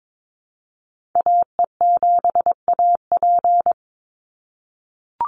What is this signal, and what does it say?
Morse code, heard as a keyed tone, about 22 wpm, 700 Hz. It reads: AE7AP
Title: NA SOTA Chaser Callsigns 22wpm